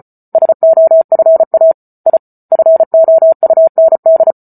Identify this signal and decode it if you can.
Morse code, heard as a keyed tone, about 34 wpm, 650 Hz. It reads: SOFA I FOUND